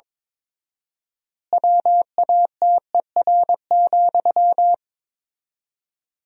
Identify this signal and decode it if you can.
Morse code, heard as a keyed tone, about 22 wpm, 700 Hz. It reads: WATER,
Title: 4x Word-Spacing - Sentences from Top 400 Words - 22wpm